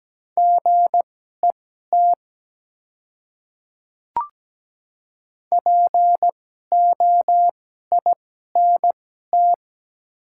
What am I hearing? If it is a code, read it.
Morse code, heard as a keyed tone, about 17 wpm, spaced as for 12 wpm, 700 Hz. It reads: GET POINT
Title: 2x Character Spacing - Top 200 Words - 17wpm